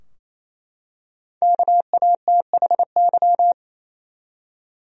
Morse code, heard as a keyed tone, about 28 wpm, 700 Hz. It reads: KATHY
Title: USA Common Names - Max 5 Letters - 28wpm